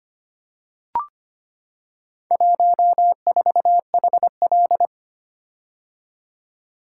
Morse code, heard as a keyed tone, about 25 wpm, 700 Hz. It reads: 14HL
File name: practice / 25wpm / Four Letter-Number Combinations - 25wpm